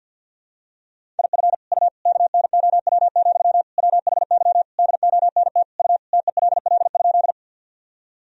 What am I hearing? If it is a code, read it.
Morse code, heard as a keyed tone, about 51 wpm, 700 Hz. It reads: IF U KNOW, WHY DONT U TELL?
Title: Sentences from Top 300 Words and Abbreviations 50wpm